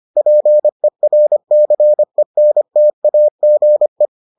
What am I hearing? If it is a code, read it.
Morse code, heard as a keyed tone, about 25 wpm, 600 Hz. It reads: PERCENTAGE